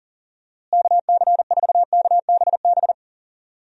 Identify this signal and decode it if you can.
Morse code, heard as a keyed tone, about 40 wpm, 700 Hz. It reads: KC4KBB
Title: USA Call Signs - 6 Characters - Repeated 3x - 40wpm